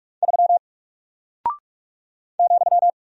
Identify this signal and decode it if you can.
Morse code, heard as a keyed tone, about 48 wpm, 700 Hz. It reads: 3 ,
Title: Single Character (Letter:Number:Punctuation:Prosign) Alt 45wpm